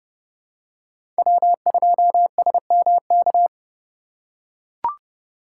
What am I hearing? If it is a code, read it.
Morse code, heard as a keyed tone, about 30 wpm, 700 Hz. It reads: W2SMK
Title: USA Call Signs - 5 Characters - 4x Character Spacing - 30wpm